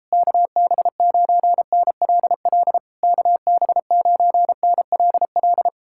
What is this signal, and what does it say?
Morse code, heard as a keyed tone, about 37 wpm, 700 Hz. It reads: KB9NLL KB9NLL